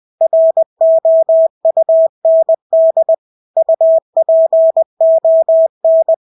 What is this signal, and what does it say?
Morse code, heard as a keyed tone, about 20 wpm, 650 Hz. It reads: ROUND UPON
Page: Engelske ord 20 wpm